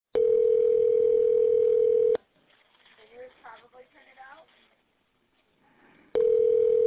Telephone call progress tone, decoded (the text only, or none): ringback tone